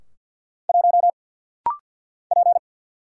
Morse code, heard as a keyed tone, about 52 wpm, 700 Hz. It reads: 1 P
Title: Single Letter-Number - Rapid-Fire 50wpm